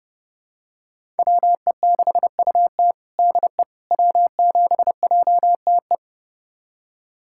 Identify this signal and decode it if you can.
Morse code, heard as a keyed tone, about 32 wpm, 700 Hz. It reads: WE6UT DE W7JTE
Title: USA Callsign DE Callsign - 5 Characters - 30wpm